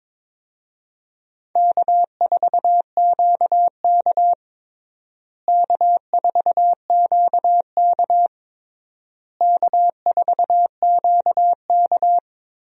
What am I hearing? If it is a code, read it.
Morse code, heard as a keyed tone, about 22 wpm, 700 Hz. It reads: K4QK K4QK K4QK
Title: USA Call Signs - 4 Characters - Repeated 3x - 22wpm